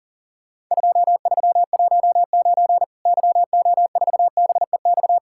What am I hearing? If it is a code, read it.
Morse code, heard as a keyed tone, about 40 wpm, 700 Hz. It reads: 2319 YO4BEX
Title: Worked All Europe DX Contest - QTC Exchange - 40wpm